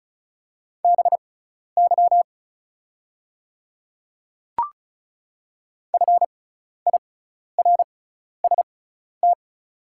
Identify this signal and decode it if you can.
Morse code, heard as a keyed tone, about 36 wpm, spaced as for 12 wpm, 690 Hz. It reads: BY FIRST
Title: 6x Character Spacing - Top 100 Words - 35wpm